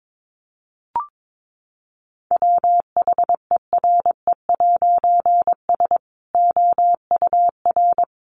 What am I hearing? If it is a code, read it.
Morse code, heard as a keyed tone, about 22 wpm, 700 Hz. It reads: WHERE'S OUR
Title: A Sound Of Thunder 22wpm